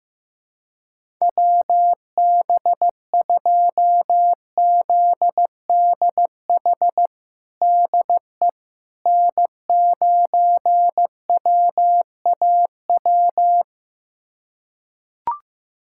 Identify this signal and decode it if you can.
Morse code, heard as a keyed tone, about 15 wpm, 700 Hz. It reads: WB2ZDH DE N9WAW
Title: USA Callsign DE Callsign 15wpm